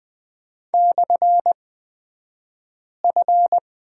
Morse code, heard as a keyed tone, about 20 wpm, spaced as for 5 wpm, 700 Hz. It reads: /F
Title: 228 Lesson 37d - Letter x - Two Characters 20wpm